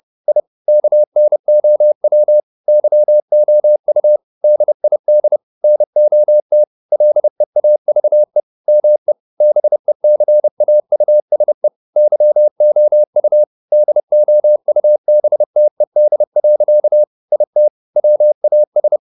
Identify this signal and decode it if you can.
Morse code, heard as a keyed tone, about 30 wpm, 600 Hz. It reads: I KNOW YOU DID NOT LEAVE ME BECAUSE YOU DOUBTED. IT WAS
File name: the old man and the sea / the old man and the sea 1 30wpm